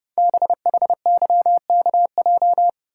Morse code, heard as a keyed tone, about 30 wpm, 700 Hz. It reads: BHYKJ
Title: Code Groups - 5 Letters - 30wpm